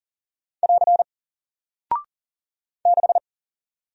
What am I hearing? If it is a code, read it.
Morse code, heard as a keyed tone, about 40 wpm, 700 Hz. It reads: <AR> 6
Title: Single Character (Letter:Number:Punctuation:Prosign) Alt 40wpm